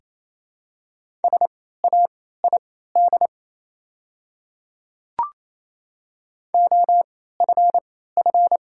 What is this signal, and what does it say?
Morse code, heard as a keyed tone, about 28 wpm, spaced as for 16 wpm, 700 Hz. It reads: SAID OFF
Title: 3x Character Spacing - Top 200 Words - 28wpm